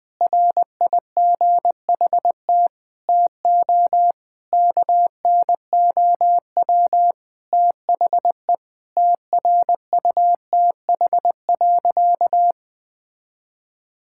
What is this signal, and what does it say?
Morse code, heard as a keyed tone, about 20 wpm, 700 Hz. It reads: RIGHT TO KNOW THE TRUTH.
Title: Sentences with 7-Word Phrases - Spoken Beginning - 20wpm